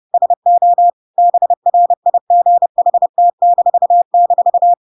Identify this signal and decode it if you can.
Morse code, heard as a keyed tone, about 30 wpm, 700 Hz. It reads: SO BRIGHT--
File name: Pinocchio / Pinocchio ch7 30 30 ik1hge